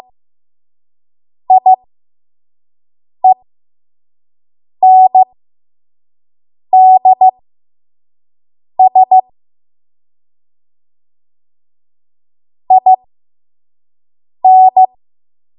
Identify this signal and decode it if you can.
Morse code, heard as a keyed tone, about 15 wpm, spaced as for 5 wpm, 750 Hz. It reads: IENDS IN
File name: Morse / 05WPM